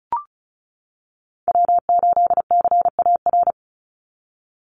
Morse code, heard as a keyed tone, about 34 wpm, 700 Hz. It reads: W8CAR
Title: Call Signs - 35wpm